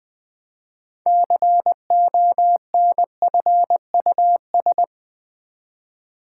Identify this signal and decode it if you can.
Morse code, heard as a keyed tone, about 20 wpm, 700 Hz. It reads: CONFUS
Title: Build A Word - 3 Letters At A Time - 8-Letter Words - 20wpm